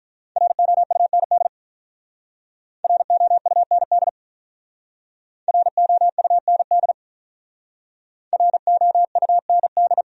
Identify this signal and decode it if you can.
Morse code, heard as a keyed tone, about 48 wpm, 700 Hz. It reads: ROUND ROUND ROUND ROUND